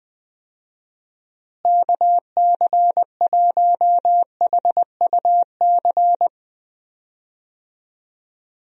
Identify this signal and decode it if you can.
Morse code, heard as a keyed tone, about 21 wpm, 700 Hz. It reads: KC1HUC
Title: USA Call Signs - 6 Characters - Repeated 2x - 20wpm